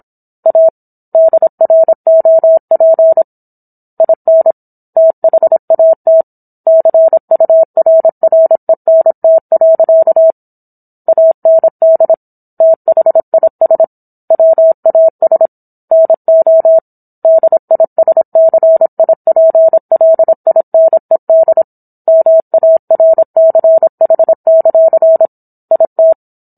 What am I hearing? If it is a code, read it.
Morse code, heard as a keyed tone, about 27 wpm, 650 Hz. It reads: A DROP IN THAT CURRENT. AND THIS WAS NO DISCIPLINED MARCH; IT
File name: Book One / 17-Chapter 17 - 26 WPM